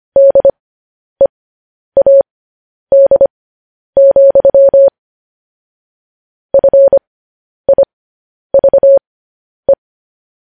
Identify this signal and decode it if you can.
Morse code, heard as a keyed tone, about 25 wpm, spaced as for 10 wpm, 550 Hz. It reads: DEAD, FIVE
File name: podcast / News headlines-1x-25x10